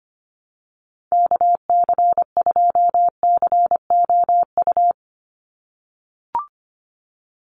Morse code, heard as a keyed tone, about 25 wpm, 700 Hz. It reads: KC2COU
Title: USA Call Signs 25wpm